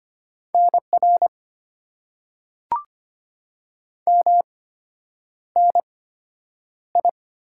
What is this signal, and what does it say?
Morse code, heard as a keyed tone, about 25 wpm, spaced as for 9 wpm, 700 Hz. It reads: NR MNI
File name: Top 90 QSO Elements - 8x Character Spacing - 25wpm